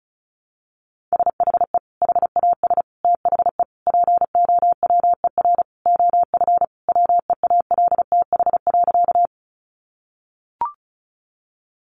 Morse code, heard as a keyed tone, about 34 wpm, 700 Hz. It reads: SHE HAS THE POWER OF WEALTH.
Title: Sentences from Top 1200 Words 35wpm